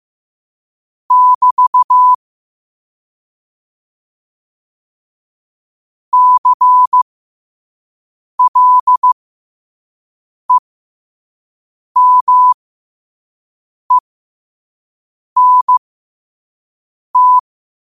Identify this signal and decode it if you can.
Morse code, heard as a keyed tone, about 15 wpm, spaced as for 5 wpm, 1000 Hz. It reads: <BT> CLEMENT
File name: qotdfiles / 5wpm